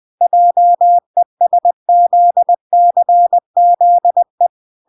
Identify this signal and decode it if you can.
Morse code, heard as a keyed tone, about 20 wpm, 700 Hz. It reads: JESZCZE